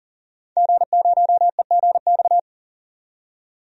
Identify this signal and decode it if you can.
Morse code, heard as a keyed tone, about 39 wpm, 700 Hz. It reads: G0EGX
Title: DXCC-100 Call Signs - 5 Characters - 40wpm